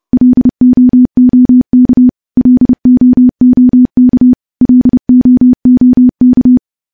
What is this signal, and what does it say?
Morse code, heard as a keyed tone, about 30 wpm, 250 Hz. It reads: LOOK LOOK LOOK
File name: morse / morse-training-30WPM